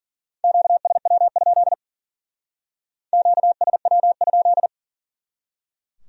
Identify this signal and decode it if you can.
Morse code, heard as a keyed tone, about 49 wpm, 700 Hz. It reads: QSW? QSW?
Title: Speed-Racing Q-Codes - 40wpm